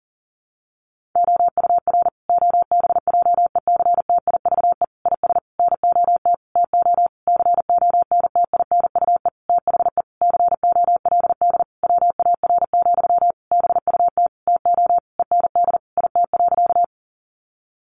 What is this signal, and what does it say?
Morse code, heard as a keyed tone, about 37 wpm, 700 Hz. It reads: OUR OBJECTIVE IS NOT TO CONTINUE THE COLD WAR, BUT TO END IT.
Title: Sentences from Top 1200 Words 40wpm